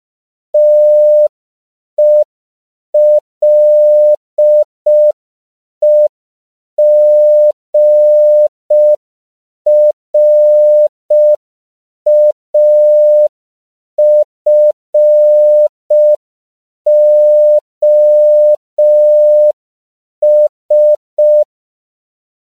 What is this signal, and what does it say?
Morse code, heard as a keyed tone, about 5 wpm, 600 Hz. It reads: TELEGRAFOS
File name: CW / 5PPM